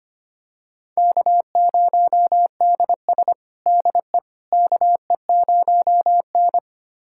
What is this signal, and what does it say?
Morse code, heard as a keyed tone, about 25 wpm, 700 Hz. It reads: K0DS DE KE0N
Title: USA Callsign DE Callsign - 4 Characters - 25wpm